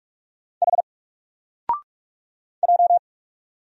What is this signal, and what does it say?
Morse code, heard as a keyed tone, about 44 wpm, 700 Hz. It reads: H J
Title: Single Letter-Number Alt 45wpm